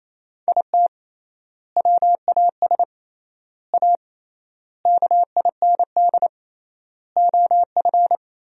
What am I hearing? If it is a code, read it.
Morse code, heard as a keyed tone, about 28 wpm, 700 Hz. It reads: IT WAS A KIND OF
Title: 3x Word-Spacing - Sentences from Top 500 Words - 28wpm